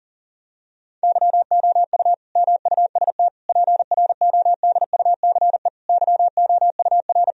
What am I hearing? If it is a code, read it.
Morse code, heard as a keyed tone, about 40 wpm, 700 Hz. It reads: YOU MUST PRODUCE YOUR
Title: Sentences from Top 1000 Words - Review Part 3 - 40wpm